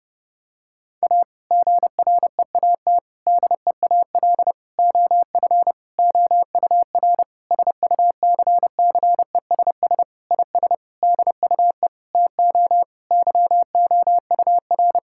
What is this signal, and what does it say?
Morse code, heard as a keyed tone, about 32 wpm, 700 Hz. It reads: A GREAT DEAL OF OUR SUCCESS IS DUE TO YOUR
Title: Sentences from Top 1000 Words - Review Part 6 - 30wpm